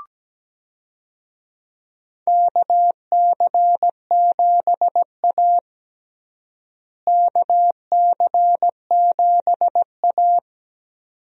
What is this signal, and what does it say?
Morse code, heard as a keyed tone, about 17 wpm, 700 Hz. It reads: KC7A KC7A
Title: USA Call Signs - 4 Characters - Repeated 2x - 17wpm